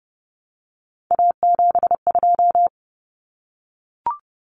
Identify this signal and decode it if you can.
Morse code, heard as a keyed tone, about 30 wpm, 700 Hz. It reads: A72
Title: Common Cut Numbers - 3-Digit 30wpm